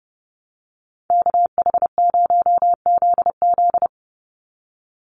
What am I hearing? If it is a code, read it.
Morse code, heard as a keyed tone, about 29 wpm, 700 Hz. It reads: KH0ZZ
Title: Call Signs Repeated 2x - 30wpm